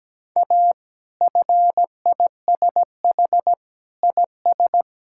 Morse code, heard as a keyed tone, about 17 wpm, 700 Hz. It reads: A FISH IS
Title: Sentences from Top 1600 Words 17wpm